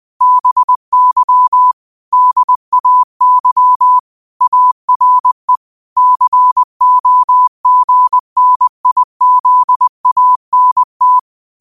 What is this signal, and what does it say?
Morse code, heard as a keyed tone, about 20 wpm, 1000 Hz. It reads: BY DAY ARE COGNIZANT
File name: qotdfiles / 20wpm